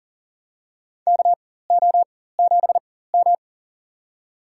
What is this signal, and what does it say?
Morse code, heard as a keyed tone, about 40 wpm, spaced as for 19 wpm, 700 Hz. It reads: KO7M
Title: USA Call Signs - 4 Characters - 4x Character Spacing - 40wpm